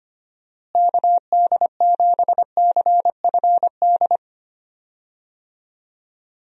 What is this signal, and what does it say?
Morse code, heard as a keyed tone, about 25 wpm, 700 Hz. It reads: KD7CFD
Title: USA Call Signs - 6 Characters - Repeated 3x - 25wpm